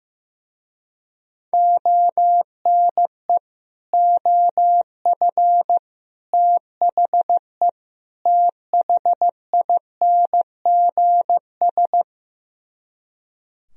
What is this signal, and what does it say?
Morse code, heard as a keyed tone, about 15 wpm, 700 Hz. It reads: ONE OF THE THINGS
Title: Speed-Racing - Sets of 4 Words 15wpm